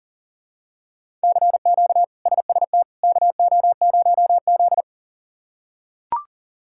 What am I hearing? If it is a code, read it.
Morse code, heard as a keyed tone, about 40 wpm, 700 Hz. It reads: CQ SST KO0Z
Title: K1USN SST QSO Practice - 40wpm